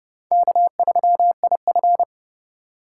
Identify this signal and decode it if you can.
Morse code, heard as a keyed tone, about 36 wpm, 700 Hz. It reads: K3IF
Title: Speed Racing - USA 1x2 Call Signs 30wpm